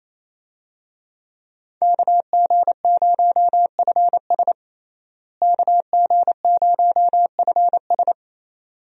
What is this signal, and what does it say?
Morse code, heard as a keyed tone, about 28 wpm, 700 Hz. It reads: KG0FS KG0FS
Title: USA Call Signs - 5 Characters - Repeated 2x - 28wpm